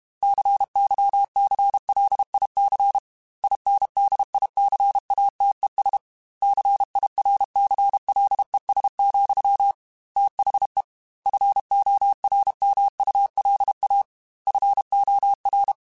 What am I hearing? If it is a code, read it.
Morse code, heard as a keyed tone, about 32 wpm, 800 Hz. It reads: CYCLIC INDICATES CIRCLES, THE FORMULA FOR
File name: Technical Subjects / Technical Subjects-mp3s / XC @30